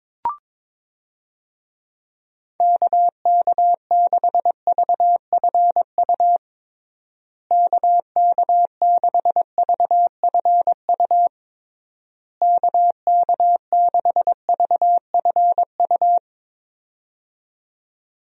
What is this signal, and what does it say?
Morse code, heard as a keyed tone, about 22 wpm, 700 Hz. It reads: KK6VFU KK6VFU KK6VFU
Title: USA Call Signs - 6 Characters - Repeated 3x - 22wpm